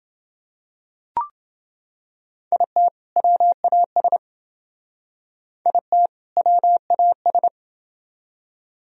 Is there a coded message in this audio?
Yes: Morse code from 2.5 to 7.5 s, heard as a keyed tone, about 29 wpm, 700 Hz. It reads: IT WAS IT WAS